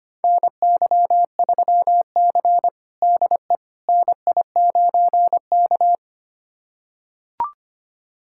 Morse code, heard as a keyed tone, about 25 wpm, 700 Hz. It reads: NY3C DE NI9K
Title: USA Callsign DE Callsign - 4 Characters - 25wpm